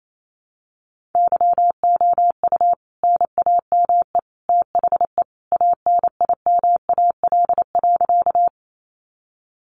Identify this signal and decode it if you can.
Morse code, heard as a keyed tone, about 28 wpm, 700 Hz. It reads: YOU NAME THE ANIMAL.
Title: A Sound Of Thunder 28wpm